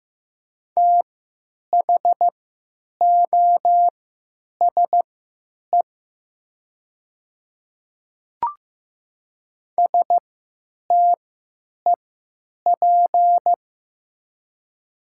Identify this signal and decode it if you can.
Morse code, heard as a keyed tone, about 15 wpm, spaced as for 9 wpm, 700 Hz. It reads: THOSE STEP
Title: 3x Character Spacing - Top 500 Words - 15wpm